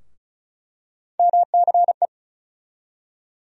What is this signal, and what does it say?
Morse code, heard as a keyed tone, about 36 wpm, 700 Hz. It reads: MCE
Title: USA Call Sign Postfixes - 3 Letters 35wpm